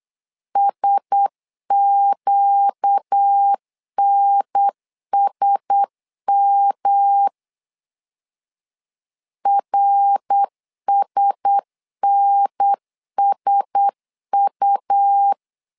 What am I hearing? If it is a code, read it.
Morse code, heard as a keyed tone, about 8 wpm, 800 Hz. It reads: SQNSM RSNSU
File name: LEZIONI / 4 Q1S 8wpm 2710